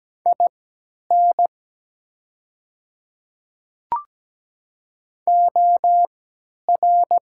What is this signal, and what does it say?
Morse code, heard as a keyed tone, about 17 wpm, spaced as for 10 wpm, 700 Hz. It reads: IN OR
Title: 3x Character Spacing - Top 100 Words - 17wpm